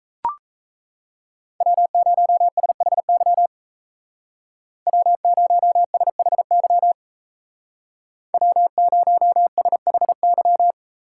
Morse code, heard as a keyed tone, about 36 wpm, 700 Hz. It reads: W0SHY W0SHY W0SHY